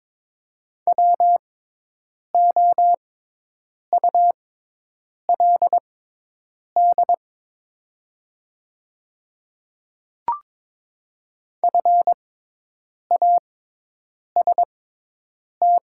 Morse code, heard as a keyed tone, about 22 wpm, spaced as for 8 wpm, 700 Hz. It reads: WOULD FAST
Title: 6x Character Spacing - Top 500 Words - 22wpm